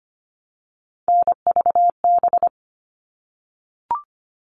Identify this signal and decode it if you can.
Morse code, heard as a keyed tone, about 25 wpm, 700 Hz. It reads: NVB
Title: All Cut Numbers - 3-Digit 25wpm